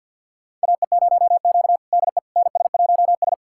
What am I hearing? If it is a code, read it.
Morse code, heard as a keyed tone, about 52 wpm, 700 Hz. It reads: AE0Q DE NS1S
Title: USA Callsign DE Callsign - 4 Characters - 50wpm